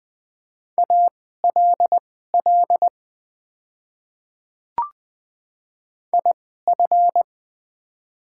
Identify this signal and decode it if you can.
Morse code, heard as a keyed tone, about 20 wpm, spaced as for 15 wpm, 700 Hz. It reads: ALL IF
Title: 2x Character Spacing - Top 100 Words - 20wpm